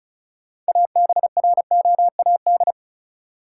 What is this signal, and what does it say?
Morse code, heard as a keyed tone, about 34 wpm, 700 Hz. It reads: ABROAD
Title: Six Letter Words - 35wpm